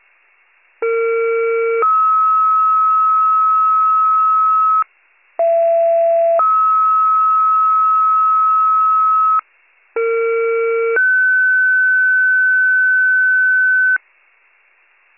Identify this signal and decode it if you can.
Two-tone page, heard as a one-second tone, then a three-second tone, about 460 then 1290 Hz, about 670 then 1290 Hz, about 460 then 1500 Hz, with noise. It is heard from 0.8 to 14.0 s.